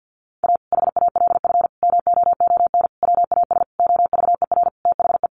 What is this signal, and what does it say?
Morse code, heard as a keyed tone, about 49 wpm, 700 Hz. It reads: A HALF MOON WAS OVER THE
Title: Sentences from Top 1100 Words 50wpm